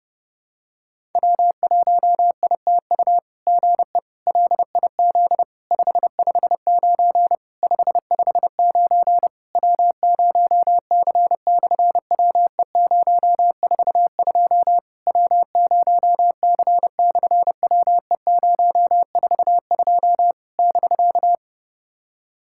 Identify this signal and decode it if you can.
Morse code, heard as a keyed tone, about 28 wpm, 700 Hz. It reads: W1ITU GE LIZ 559 559 W0C/WE042 W0C/WE042 <BK>
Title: SOTA QSO Practice 30wpm